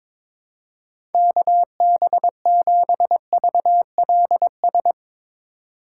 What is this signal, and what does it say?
Morse code, heard as a keyed tone, about 22 wpm, 700 Hz. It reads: KB7VLS
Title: USA Call Signs - 6 Characters - 2x Character Spacing - 22wpm